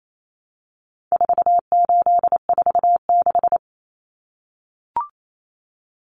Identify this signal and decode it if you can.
Morse code, heard as a keyed tone, about 28 wpm, 700 Hz. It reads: 4846
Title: Common Cut Numbers - 4-Digit 28wpm